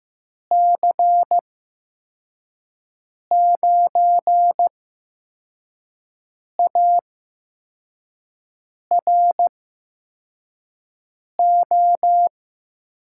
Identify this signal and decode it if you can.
Morse code, heard as a keyed tone, about 15 wpm, spaced as for 4 wpm, 700 Hz. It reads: C9ARO
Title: USA Call Signs - 6 Characters - 8x Character Spacing - 15wpm